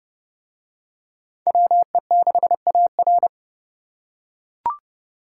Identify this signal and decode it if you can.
Morse code, heard as a keyed tone, about 29 wpm, 700 Hz. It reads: WE6AR